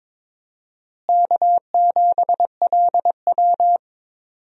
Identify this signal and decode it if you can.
Morse code, heard as a keyed tone, about 22 wpm, 700 Hz. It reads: K7LW
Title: USA Call Signs - 4 Characters - Repeated 3x - 22wpm